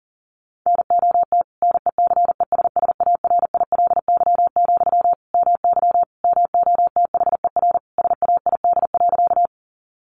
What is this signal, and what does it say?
Morse code, heard as a keyed tone, about 40 wpm, 700 Hz. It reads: NOT NECESSARILY, MY MOTHER SAID.
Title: Sentences from Top 1400 Words 40wpm